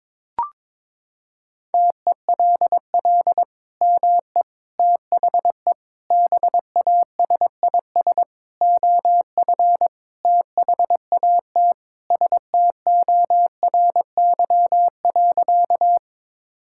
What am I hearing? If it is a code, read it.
Morse code, heard as a keyed tone, about 22 wpm, 700 Hz. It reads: TELL ME THE BASIS OF THAT STORY.
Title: Sentences from Top 1000 Words - Review Part 5 - 22wpm